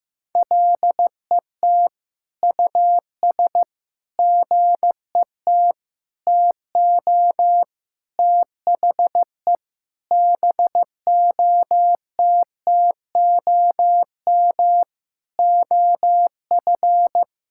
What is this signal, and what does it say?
Morse code, heard as a keyed tone, about 16 wpm, 700 Hz. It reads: LET US GET TO THE BOTTOM OF
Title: Sentences from Top 1000 Words - Review Part 6 - 15wpm